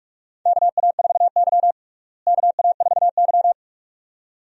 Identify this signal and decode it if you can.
Morse code, heard as a keyed tone, about 46 wpm, 700 Hz. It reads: KA4Y KA4Y
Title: USA Call Signs - 4 Characters - Repeated 2x - 45wpm